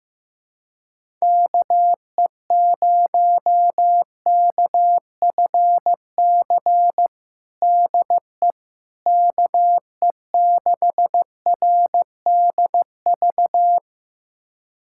Morse code, heard as a keyed tone, about 15 wpm, 700 Hz. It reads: KE0KFC DE KE6RDV
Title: USA Callsign DE Callsign - 6 Characters - 15wpm